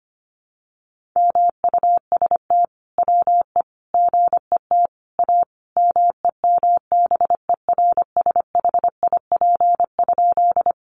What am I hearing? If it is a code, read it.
Morse code, heard as a keyed tone, about 25 wpm, 700 Hz. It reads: MUST WE GET A MEMBERSHIP?
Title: Sentences from Top 1400 Words 25wpm